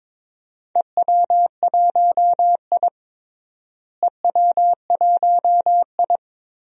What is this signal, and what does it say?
Morse code, heard as a keyed tone, about 22 wpm, 700 Hz. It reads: EW1I EW1I
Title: DXCC-100 Call Signs - 4 Characters - Repeated 3x - 22wpm